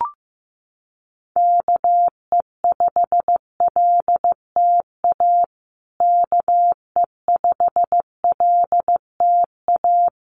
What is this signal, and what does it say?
Morse code, heard as a keyed tone, about 14 wpm, 700 Hz. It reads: KE5LTA KE5LTA
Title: USA Call Signs Repeated 3x 15wpm